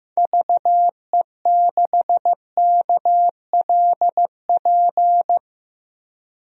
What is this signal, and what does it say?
Morse code, heard as a keyed tone, about 15 wpm, 700 Hz. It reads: VE6KLP
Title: DXCC-100 Call Signs - 6 Characters - Repeated 2x - 15wpm